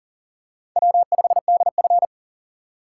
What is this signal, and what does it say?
Morse code, heard as a keyed tone, about 40 wpm, 700 Hz. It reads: W5DF
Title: USA Call Signs - 4 Characters - 4x Character Spacing - 40wpm